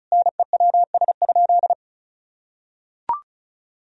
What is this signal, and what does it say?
Morse code, heard as a keyed tone, about 34 wpm, 700 Hz. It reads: NEWS?
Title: Sentences from Top 1000 Words - Review Part 4 - 35wpm